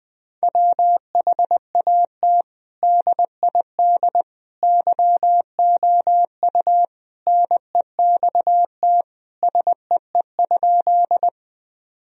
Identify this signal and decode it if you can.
Morse code, heard as a keyed tone, about 20 wpm, 700 Hz. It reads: WHAT DID YOU NEXT SEE?
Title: Sentences from Top 1000 Words - Review Part 1 - 20wpm